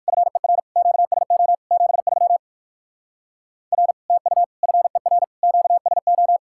Morse code, heard as a keyed tone, about 59 wpm, 700 Hz. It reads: FER QSO 73 R TU FER QSO